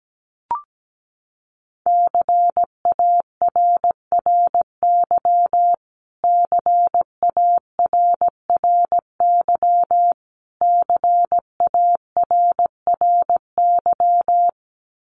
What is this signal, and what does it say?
Morse code, heard as a keyed tone, about 17 wpm, 700 Hz. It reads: CARRY CARRY CARRY
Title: Top 500 Words Repeated 17wpm